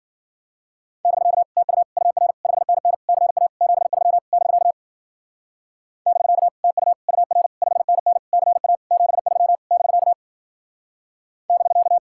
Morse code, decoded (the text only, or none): <BK> TU UR 5NN CA 73 <BK> <BK> TU UR 5NN CA 73 <BK> <BK>